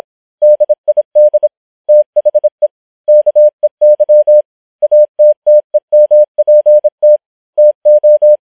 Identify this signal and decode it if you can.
Morse code, heard as a keyed tone, about 26 wpm, 600 Hz. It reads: DID THE KEY ATTEMPT TO